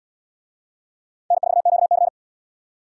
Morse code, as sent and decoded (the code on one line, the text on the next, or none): -. ....- --.- -.-.
N4QC